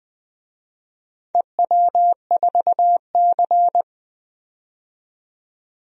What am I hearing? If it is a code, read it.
Morse code, heard as a keyed tone, about 20 wpm, 700 Hz. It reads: EW4C
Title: DXCC-100 Call Signs - 4 Characters - 20wpm